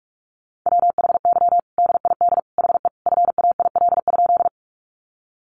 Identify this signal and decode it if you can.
Morse code, heard as a keyed tone, about 44 wpm, 700 Hz. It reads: WHY DID HE FAIL?
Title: Sentences from Top 1100 Words 45wpm